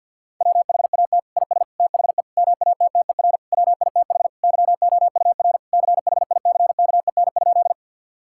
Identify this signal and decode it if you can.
Morse code, heard as a keyed tone, about 50 wpm, 700 Hz. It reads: WHAT IS THE MATTER WITH YOUR CHICKEN?
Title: Sentences from Top 1000 Words - Review Part 5 - 50wpm